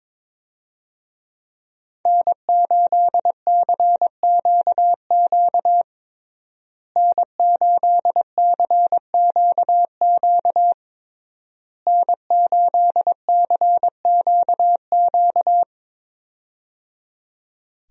Morse code, heard as a keyed tone, about 22 wpm, 700 Hz. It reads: N8CQQ N8CQQ N8CQQ
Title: USA Call Signs - 5 Characters - Repeated 3x - 22wpm